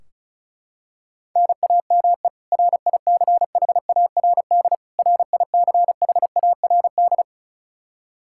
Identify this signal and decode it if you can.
Morse code, heard as a keyed tone, about 36 wpm, 700 Hz. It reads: NAME RICHARD RICHARD
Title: Practice QSO - Exchange 1 - Beginning - 35wpm